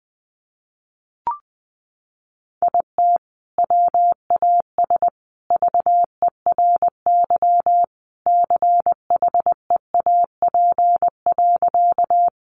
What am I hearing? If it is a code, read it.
Morse code, heard as a keyed tone, about 20 wpm, 700 Hz. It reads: IT WAS VERY CHEAP.
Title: Sentences from Top 1200 Words 20wpm